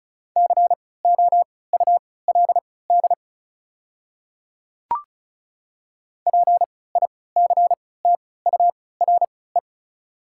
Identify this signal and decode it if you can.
Morse code, heard as a keyed tone, about 36 wpm, spaced as for 20 wpm, 700 Hz. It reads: COULD PICTURE
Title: 3x Character Spacing - Top 200 Words - 35wpm